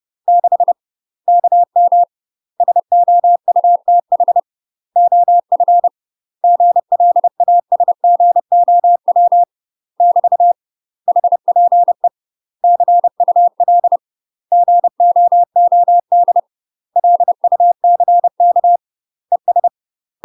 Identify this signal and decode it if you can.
Morse code, heard as a keyed tone, about 30 wpm, 700 Hz. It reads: 6 KM SOUTH OF GLASGOW <BT> HPE CUL GOOD LUCK ES
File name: UK MASTER QSOs TRAINING / 30WPM / UK MASTER QSOs 0028